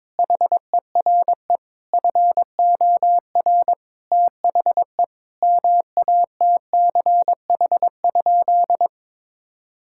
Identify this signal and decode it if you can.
Morse code, heard as a keyed tone, about 22 wpm, 700 Hz. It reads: HERE FOR THE MATCH?
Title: Sentences from Top 1600 Words 22wpm